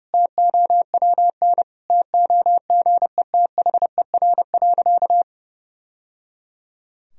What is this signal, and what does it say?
Morse code, heard as a keyed tone, about 30 wpm, 700 Hz. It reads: TOWN TOGETHER.
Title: Sentences from Top 1000 Words - Review Part 2 - 30wpm